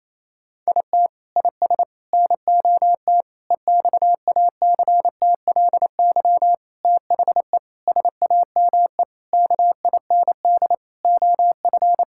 Morse code, heard as a keyed tone, about 27 wpm, 700 Hz. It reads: IT IS NOT EXACTLY THE SAME KIND OF
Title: Sentences from Top 1000 Words - Review Part 6 - 28wpm